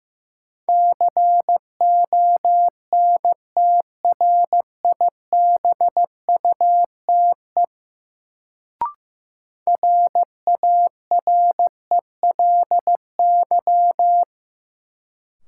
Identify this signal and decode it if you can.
Morse code, heard as a keyed tone, about 15 wpm, 700 Hz. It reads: CONTRIBUTE RARELY
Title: Sentences from Top 1400 Words - New Word Drill - 15wpm